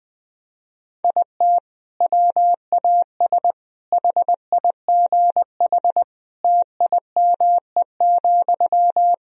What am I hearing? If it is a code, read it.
Morse code, heard as a keyed tone, about 20 wpm, 700 Hz. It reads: IT WAS HIGH TIME,
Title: Sentences from Top 1000 Words - Review Part 1 - 20wpm